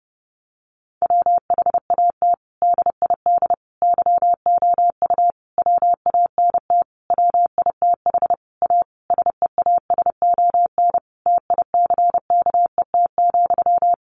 Morse code, heard as a keyed tone, about 30 wpm, 700 Hz. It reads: WHAT DID YOU WANT WITH A SEASON TICKET,
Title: Sentences from Top 1400 Words 30wpm